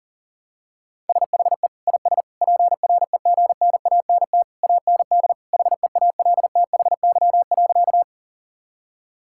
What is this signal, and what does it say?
Morse code, heard as a keyed tone, about 40 wpm, 700 Hz. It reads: SHE IS PREGNANT AND HEALTHY.
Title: Sentences from Top 1000 Words - Review Part 6 - 40wpm